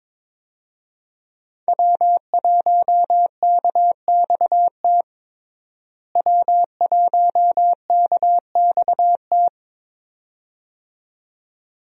Morse code, heard as a keyed tone, about 22 wpm, 700 Hz. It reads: W1KXT W1KXT
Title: USA Call Signs - 5 Characters - Repeated 2x - 22wpm